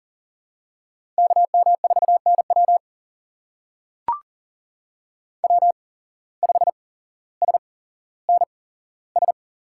Morse code, heard as a keyed tone, about 39 wpm, spaced as for 17 wpm, 700 Hz. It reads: KM4NW W5SNS
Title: USA Call Signs - 5 Characters - 8x Character Spacing - 40wpm